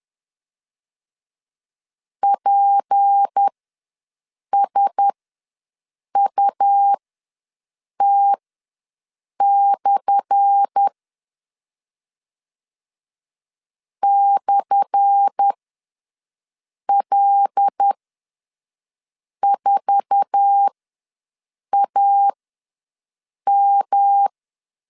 Morse code, heard as a keyed tone, about 10 wpm, spaced as for 5 wpm, 800 Hz. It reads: PSUT/ /L4AM
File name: LEZIONI / 10 SIGNS / 10A SLASH ws 10wpm 4121